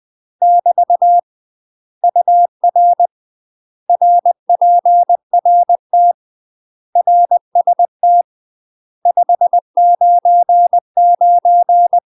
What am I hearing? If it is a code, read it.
Morse code, heard as a keyed tone, about 20 wpm, 700 Hz. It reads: <BT> UR RPRT RST 599
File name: UK MASTER QSOs TRAINING / 20WPM / UK MASTER QSOs 0001